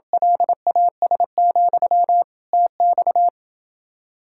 Morse code, heard as a keyed tone, about 26 wpm, 700 Hz. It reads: LAS, TX